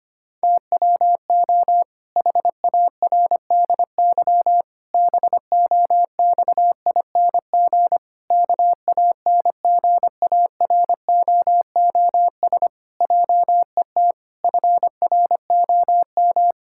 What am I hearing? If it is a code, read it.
Morse code, heard as a keyed tone, about 25 wpm, 700 Hz. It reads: TWO HARDY BOXING KANGAROOS JET FROM
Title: Pangram Challenge v2 - 25wpm